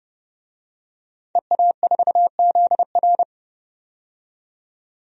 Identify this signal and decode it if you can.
Morse code, heard as a keyed tone, about 32 wpm, 700 Hz. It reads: EA4ZR
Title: DXCC-100 Call Signs - 5 Characters - 30wpm